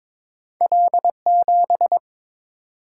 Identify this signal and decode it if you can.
Morse code, heard as a keyed tone, about 23 wpm, 700 Hz. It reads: L7
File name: practice / 22wpm / Two Letter-Number Combinations - 22wpm